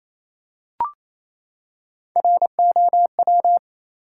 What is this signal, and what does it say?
Morse code, heard as a keyed tone, about 28 wpm, 700 Hz. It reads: ROW
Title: Three Letter Words - 28wpm